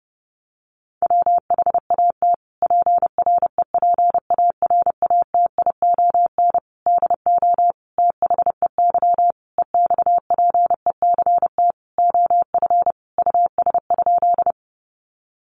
Morse code, heard as a keyed tone, about 30 wpm, 700 Hz. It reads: WHAT PREPARATION DO THEY EXPECT OF US?
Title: Sentences from Top 1400 Words 30wpm